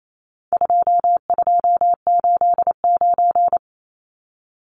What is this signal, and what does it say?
Morse code, heard as a keyed tone, about 28 wpm, 700 Hz. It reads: 2289
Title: Numbers - 4-Digit 28wpm